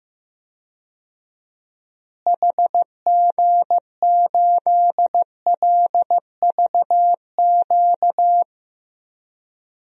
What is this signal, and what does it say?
Morse code, heard as a keyed tone, about 16 wpm, 700 Hz. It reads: HG8LVQ
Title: DXCC-100 Call Signs - 6 Characters - Repeated 3x - 15wpm